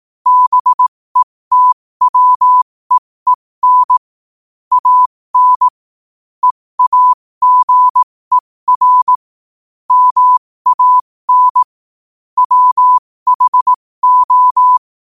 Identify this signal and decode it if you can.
Morse code, heard as a keyed tone, about 18 wpm, spaced as for 15 wpm, 1000 Hz. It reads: BETWEEN AN EAGER MAN WHO
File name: qotdfiles / 15wpm